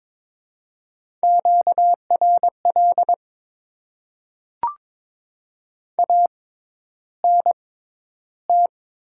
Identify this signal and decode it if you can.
Morse code, heard as a keyed tone, about 22 wpm, spaced as for 11 wpm, 690 Hz. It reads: QRL ANT
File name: Top 20 QSO Elements - 6x Character Spacing - 22wpm